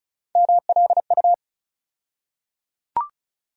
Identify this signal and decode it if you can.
Morse code, heard as a keyed tone, about 34 wpm, 700 Hz. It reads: MLU
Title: USA Call Sign Postfixes - 3 Letters 35wpm